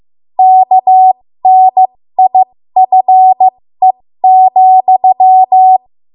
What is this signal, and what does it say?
Morse code, heard as a keyed tone, about 15 wpm, spaced as for 13 wpm, 750 Hz. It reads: KNIFE,
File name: Morse / 13WPM